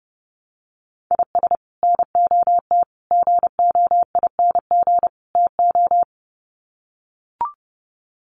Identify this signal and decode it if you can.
Morse code, heard as a keyed tone, about 29 wpm, 700 Hz. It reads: IS NOT GOING TO
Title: Sets of 4 Words 30wpm